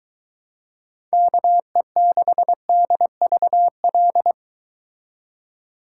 Morse code, heard as a keyed tone, about 23 wpm, 700 Hz. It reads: KE6DVL